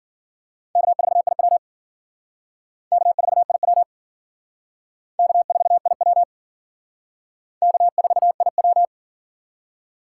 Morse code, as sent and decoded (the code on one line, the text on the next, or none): -.- ....- .. .-- / -.- ....- .. .-- / -.- ....- .. .-- / -.- ....- .. .--
K4IW K4IW K4IW K4IW